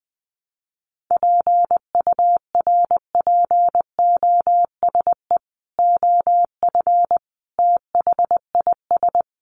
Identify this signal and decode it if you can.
Morse code, heard as a keyed tone, about 20 wpm, 700 Hz. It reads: PURPOSE OF THIS
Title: Sets of 5 Words 20wpm